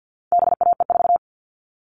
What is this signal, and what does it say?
Morse code, heard as a keyed tone, about 50 wpm, 700 Hz. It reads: BAE4
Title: All Cut Numbers - 4-Digit 50wpm